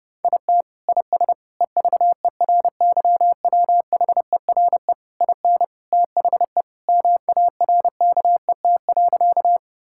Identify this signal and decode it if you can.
Morse code, heard as a keyed tone, about 30 wpm, 700 Hz. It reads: IT IS EVERYWHERE IN THE MARKET.
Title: Sentences from Top 1600 Words 30wpm